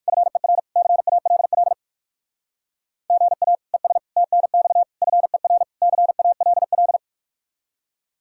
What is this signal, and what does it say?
Morse code, heard as a keyed tone, about 68 wpm, spaced as for 57 wpm, 700 Hz. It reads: FER CALL GA ES TNX FER CALL